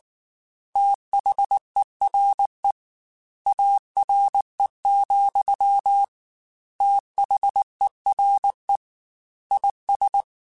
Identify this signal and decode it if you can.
Morse code, heard as a keyed tone, about 19 wpm, 800 Hz. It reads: THERE ARE, THERE IS
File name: Technical Subjects / Technical Subjects-mp3s / AND Gate @18